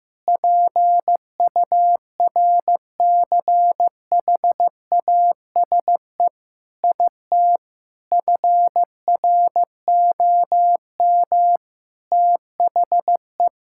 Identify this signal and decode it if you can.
Morse code, heard as a keyed tone, about 15 wpm, 700 Hz. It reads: PURCHASE IT FROM THE
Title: Sentences from Top 1000 Words - Review Part 5 - 15wpm